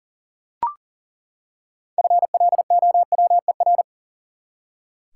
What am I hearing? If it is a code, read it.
Morse code, heard as a keyed tone, about 37 wpm, 700 Hz. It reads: FLOWER
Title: Six Letter Words - 40wpm